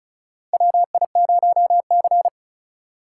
Morse code, heard as a keyed tone, about 36 wpm, 700 Hz. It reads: WI0C
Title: USA Call Signs - 4 Characters - 4x Character Spacing - 35wpm